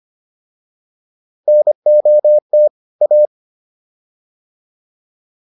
Morse code, heard as a keyed tone, about 25 wpm, 600 Hz. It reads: NOT A